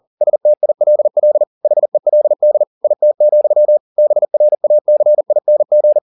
Morse code, heard as a keyed tone, about 40 wpm, 600 Hz. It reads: STILL HELD IT, BRACING